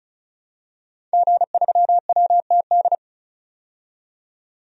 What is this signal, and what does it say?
Morse code, heard as a keyed tone, about 34 wpm, 700 Hz. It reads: G3WTD
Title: DXCC-100 Call Signs - 5 Characters - 35wpm